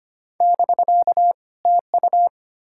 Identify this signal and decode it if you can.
Morse code, heard as a keyed tone, about 33 wpm, 700 Hz. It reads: <BK> TU